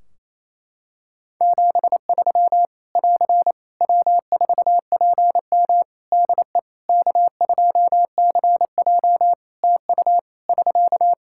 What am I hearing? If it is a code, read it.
Morse code, heard as a keyed tone, about 27 wpm, 700 Hz. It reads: 73 <AR> W4PM DE K2CJ TU <SK>
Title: K1USN SST QSO Practice 28wpm